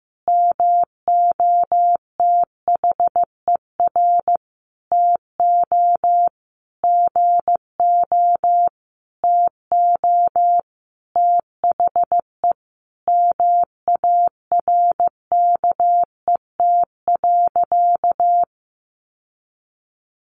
Morse code, heard as a keyed tone, about 15 wpm, 700 Hz. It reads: MOTHER TO GO TO THE MARKET.
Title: Sentences from Top 1400 Words 15wpm